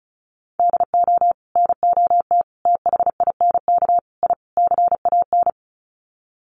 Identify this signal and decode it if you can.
Morse code, heard as a keyed tone, about 34 wpm, 700 Hz. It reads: DO NOT THINK I CAN
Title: Sets of 5 Words Encore2 35wpm